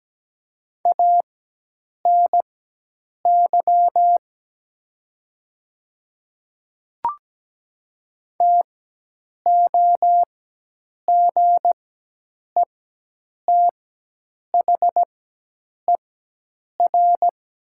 Morse code, heard as a keyed tone, about 17 wpm, spaced as for 8 wpm, 700 Hz. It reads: ANY TOGETHER
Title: 4x Character Spacing - Top 300 Words - 17wpm